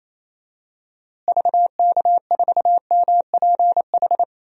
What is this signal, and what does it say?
Morse code, heard as a keyed tone, about 28 wpm, 700 Hz. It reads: VK4MPH
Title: DXCC-100 Call Signs - 6 Characters - Repeated 3x - 28wpm